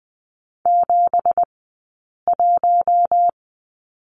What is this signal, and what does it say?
Morse code, heard as a keyed tone, about 20 wpm, spaced as for 8 wpm, 700 Hz. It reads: 71
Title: 211 Lesson 34d - Number 8 - Two Characters 20wpm